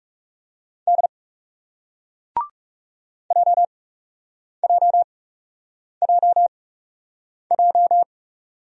Morse code, heard as a keyed tone, about 39 wpm, spaced as for 8 wpm, 700 Hz. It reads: D JJJJ